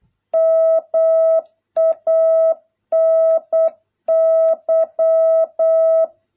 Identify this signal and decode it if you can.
Morse code, heard as a keyed tone, about 8 wpm, 650 Hz. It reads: MANY